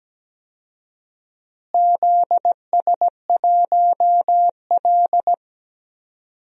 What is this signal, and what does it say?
Morse code, heard as a keyed tone, about 17 wpm, 700 Hz. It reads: ZS1L
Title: DXCC-100 Call Signs - 4 Characters - Repeated 2x - 17wpm